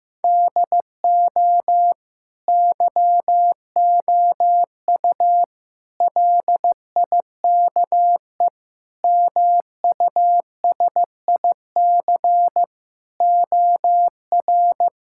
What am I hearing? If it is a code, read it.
Morse code, heard as a keyed tone, about 15 wpm, 700 Hz. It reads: DO YOU LIKE MUSIC OR
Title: Sentences from Top 1000 Words - Review Part 6 - 15wpm